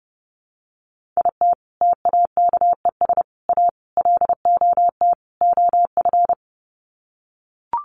Morse code, heard as a keyed tone, about 30 wpm, 700 Hz. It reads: IT TAKES A LOT OF
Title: Sets of 5 Words Encore 30wpm